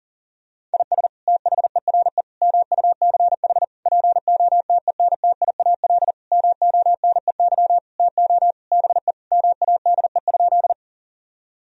Morse code, heard as a keyed tone, about 40 wpm, 700 Hz. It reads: IS THERE MUCH POTENTIAL MONEY TO BE MADE?
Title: Sentences from Top 1000 Words - Review Part 4 - 40wpm